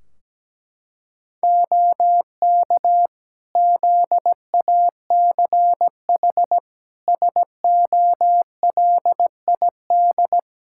Morse code, heard as a keyed tone, about 17 wpm, 700 Hz. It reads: OK ZACH SOLID
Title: K1USN SST QSO Practice 17wpm